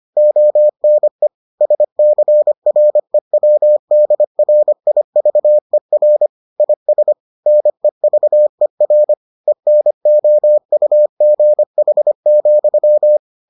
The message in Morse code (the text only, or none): ONE SCREWDRIVER IS NEVER ENOUGH,